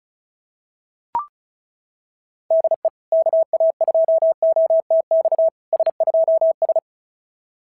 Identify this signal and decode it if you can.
Morse code, heard as a keyed tone, about 36 wpm, 660 Hz. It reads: DE KA2OTX S2S
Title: SOTA QSO Practice 35wpm